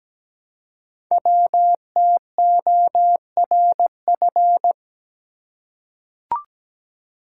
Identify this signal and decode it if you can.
Morse code, heard as a keyed tone, about 17 wpm, 700 Hz. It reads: WTORF
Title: Code Groups - 5 Letter:Numbers - 17wpm